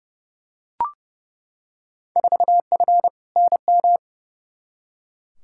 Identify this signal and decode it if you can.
Morse code, heard as a keyed tone, about 32 wpm, 700 Hz. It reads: 4F NM
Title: Field Day Exchanges Encore - 30wpm